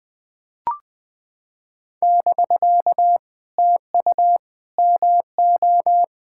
Morse code, heard as a keyed tone, about 20 wpm, 700 Hz. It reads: <BK> TU MO
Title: POTA QSO Practice - 20wpm